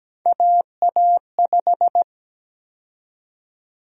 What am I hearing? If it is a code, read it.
Morse code, heard as a keyed tone, about 17 wpm, 700 Hz. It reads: AA5
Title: USA Call Sign Prefixes - 17wpm